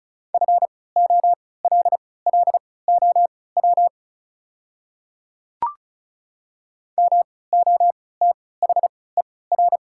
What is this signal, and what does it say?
Morse code, heard as a keyed tone, about 36 wpm, spaced as for 20 wpm, 700 Hz. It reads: FOLLOW MOTHER
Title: 3x Character Spacing - Top 200 Words - 35wpm